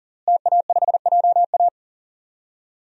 Morse code, heard as a keyed tone, about 40 wpm, 700 Hz. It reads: TA5JA
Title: DXCC-100 Call Signs - 5 Characters - Repeated 2x - 40wpm